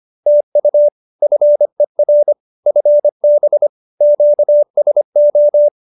Morse code, heard as a keyed tone, about 25 wpm, 590 Hz. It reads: TU FER FB QSO